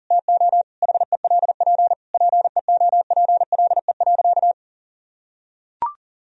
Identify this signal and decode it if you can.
Morse code, heard as a keyed tone, about 37 wpm, 700 Hz. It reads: TO HELP PEOPLE.
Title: Sentences from Top 1000 Words - Review Part 4 - 40wpm